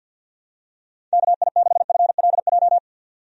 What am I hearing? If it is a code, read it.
Morse code, heard as a keyed tone, about 50 wpm, 700 Hz. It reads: KI6FLJ
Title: USA Call Signs - 6 Characters - 50wpm